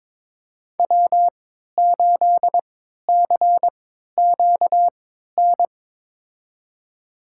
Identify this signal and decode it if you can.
Morse code, heard as a keyed tone, about 22 wpm, spaced as for 12 wpm, 700 Hz. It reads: W8CQN
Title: USA Call Signs - 5 Characters - 3x Character Spacing - 22wpm